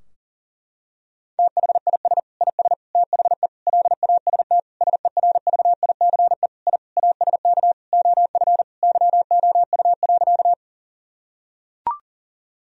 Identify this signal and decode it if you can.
Morse code, heard as a keyed tone, about 40 wpm, 700 Hz. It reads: THIS IS THE LAST SERVICE I ASK OF YOU.
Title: Sentences from Top 1000 Words - Review Part 4 - 40wpm